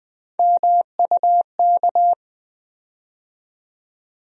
Morse code, heard as a keyed tone, about 20 wpm, 700 Hz. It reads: MUK
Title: USA Call Sign Postfixes - 3 Letters 20wpm